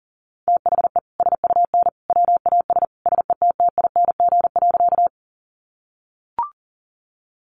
Morse code, heard as a keyed tone, about 40 wpm, 700 Hz. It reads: THE SUN WAS SETTING.
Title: Sentences from Top 1300 Words 40wpm